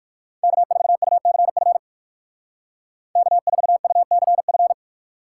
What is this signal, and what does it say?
Morse code, heard as a keyed tone, about 55 wpm, 700 Hz. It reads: K4UCF K4UCF
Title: Speed Racing - USA 1x3 Call Signs 45wpm